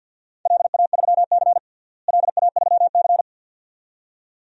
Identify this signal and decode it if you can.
Morse code, heard as a keyed tone, about 50 wpm, 700 Hz. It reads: LA3C LA3C
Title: DXCC-100 Call Signs - 4 Characters - Repeated 2x - 50wpm